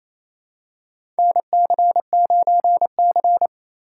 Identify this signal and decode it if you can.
Morse code, heard as a keyed tone, about 28 wpm, 700 Hz. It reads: NC9C
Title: USA Call Signs - 4 Characters - 4x Character Spacing - 28wpm